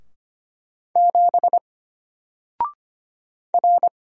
Morse code, heard as a keyed tone, about 25 wpm, 700 Hz. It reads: 7 R
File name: Single-Letter-Number-25wpm